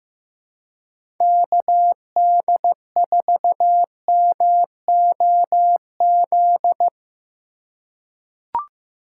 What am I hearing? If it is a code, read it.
Morse code, heard as a keyed tone, about 15 wpm, 700 Hz. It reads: KD4MOZ
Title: USA Call Signs - 6 Characters - 2x Character Spacing - 15wpm